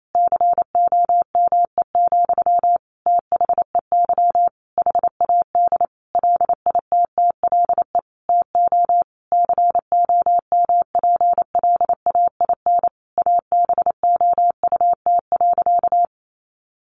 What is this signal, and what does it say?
Morse code, heard as a keyed tone, about 28 wpm, 700 Hz. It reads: COME, THEY HAD LITTLE TO COMPLAIN ABOUT.
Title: Sentences from Top 1300 Words 28wpm